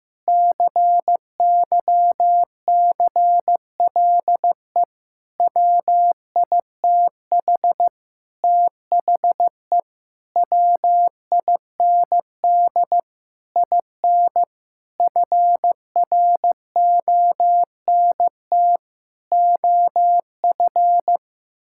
Morse code, heard as a keyed tone, about 15 wpm, 700 Hz. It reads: CYCLE WITH THE WIND IN FRONT OF
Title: Sentences from Top 1000 Words - Review Part 6 - 15wpm